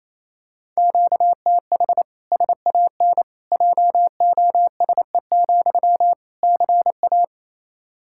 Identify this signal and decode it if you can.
Morse code, heard as a keyed tone, about 27 wpm, 700 Hz. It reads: QTH SAN JOSE, CA
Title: Practice QSO - Exchange 1 - Beginning - 28wpm